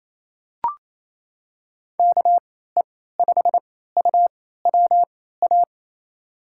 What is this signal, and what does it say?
Morse code, heard as a keyed tone, about 28 wpm, spaced as for 16 wpm, 700 Hz. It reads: KE5UWA
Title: USA Call Signs - 6 Characters - 3x Character Spacing - 28wpm